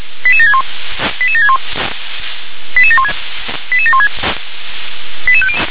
Five-tone selective call (ZVEI), heard as quick runs of five tones, tones of about 70 ms. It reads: 88761 88761 88716 88716
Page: SDR/ZVEI